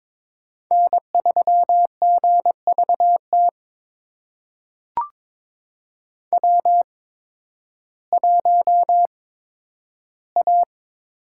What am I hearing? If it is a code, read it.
Morse code, heard as a keyed tone, about 21 wpm, spaced as for 12 wpm, 700 Hz. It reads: N3GVT W1A
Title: USA Call Signs - 5 Characters - 8x Character Spacing - 22wpm